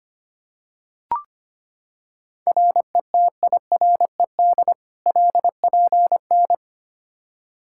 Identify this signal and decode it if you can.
Morse code, heard as a keyed tone, about 25 wpm, 700 Hz. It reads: RETIRED LPN
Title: Practice QSO - Exchange 3 - More Info - 25wpm